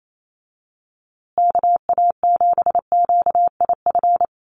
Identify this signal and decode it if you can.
Morse code, heard as a keyed tone, about 28 wpm, 700 Hz. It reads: KA7QIF
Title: USA Call Signs Repeated 2x 28wpm